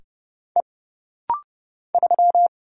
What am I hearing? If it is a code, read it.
Morse code, heard as a keyed tone, about 30 wpm, 700 Hz. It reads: E 3
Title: Single Letter-Number Alt - Rapid-Fire 30wpm